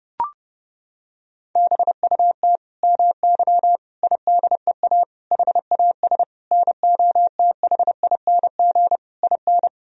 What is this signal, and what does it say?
Morse code, heard as a keyed tone, about 30 wpm, 700 Hz. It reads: BUT MY IDEA HAS NOTHING IN
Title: Sentences from Top 1000 Words - Review Part 3 - 30wpm